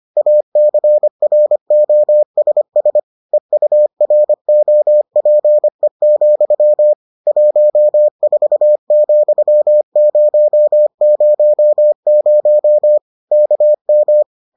Morse code, heard as a keyed tone, about 25 wpm, 600 Hz. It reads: ACROSS EUROPE, 14,000 KM